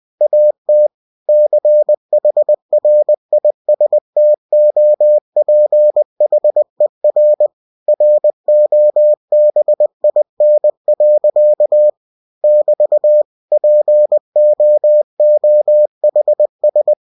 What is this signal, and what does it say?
Morse code, heard as a keyed tone, about 20 wpm, 600 Hz. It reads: AT CHRISTOPHER ROBIN. <BT> POOHS